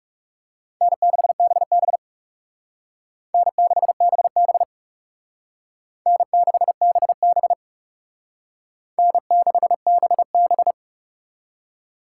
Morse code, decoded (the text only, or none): N6BB N6BB N6BB N6BB